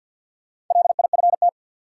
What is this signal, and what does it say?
Morse code, heard as a keyed tone, about 53 wpm, 700 Hz. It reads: LIFT
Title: Sentences from Top 1000 Words - New Word Drill - 50wpm